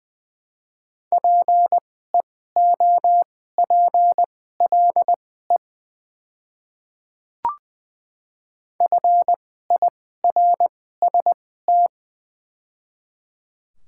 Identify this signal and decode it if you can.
Morse code, heard as a keyed tone, about 21 wpm, spaced as for 15 wpm, 700 Hz. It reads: PEOPLE FIRST
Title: 2x Character Spacing - Top 100 Words - 20wpm